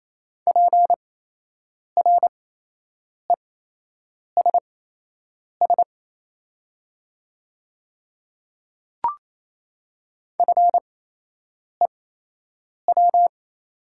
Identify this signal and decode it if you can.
Morse code, heard as a keyed tone, about 27 wpm, spaced as for 8 wpm, 700 Hz. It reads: PRESS FEW
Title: 8x Character Spacing - Top 300 Words - 28wpm